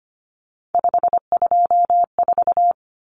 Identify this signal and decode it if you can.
Morse code, heard as a keyed tone, about 25 wpm, 700 Hz. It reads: 524
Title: Common Cut Numbers - 3-Digit 25wpm